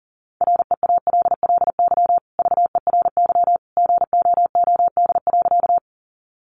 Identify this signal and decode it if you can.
Morse code, heard as a keyed tone, about 40 wpm, 700 Hz. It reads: REALLY VERY GOOD.
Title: Sentences from Top 600 Words 40wpm